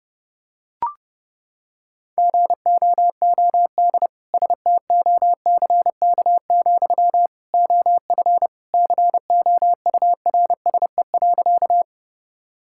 Morse code, heard as a keyed tone, about 30 wpm, 700 Hz. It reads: GOOD STOCK, OF COURSE.
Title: Sentences from Top 1000 Words - Review Part 5 - 30wpm